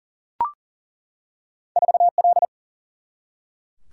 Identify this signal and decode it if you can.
Morse code, heard as a keyed tone, about 40 wpm, 700 Hz. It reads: 4L
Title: Single-Letter Single-Number 40wpm